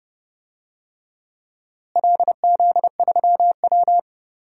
Morse code, heard as a keyed tone, about 29 wpm, 700 Hz. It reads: LZ3W
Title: DXCC-100 Call Signs - 4 Characters - Repeated 3x - 30wpm